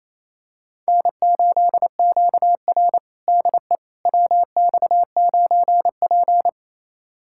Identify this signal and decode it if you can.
Morse code, heard as a keyed tone, about 27 wpm, 690 Hz. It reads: N8QR DE WX9P
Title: USA Callsign DE Callsign - 4 Characters - 28wpm